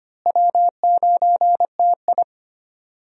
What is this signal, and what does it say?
Morse code, heard as a keyed tone, about 25 wpm, 700 Hz. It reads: W9TI
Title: USA Call Signs - 4 Characters - Repeated 3x - 25wpm